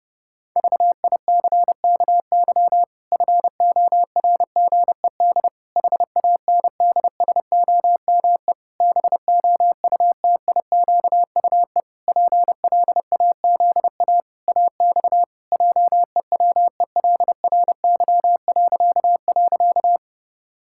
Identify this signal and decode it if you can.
Morse code, heard as a keyed tone, about 30 wpm, 700 Hz. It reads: VICKY FORGED HANDSOME BOUTIQUE PLAZA AX JEWELRY..
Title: Pangram Challenge v2 - 30wpm